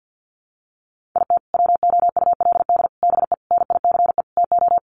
Morse code, heard as a keyed tone, about 50 wpm, 700 Hz. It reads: IT WOULD BE NICE TO
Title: Sets of 5 Words Encore2 50wpm